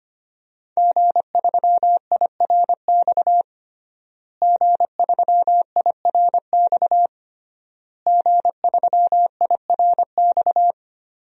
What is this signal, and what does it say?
Morse code, heard as a keyed tone, about 25 wpm, 700 Hz. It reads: G3IRX G3IRX G3IRX
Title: DXCC-100 Call Signs - 5 Characters - Repeated 3x - 25wpm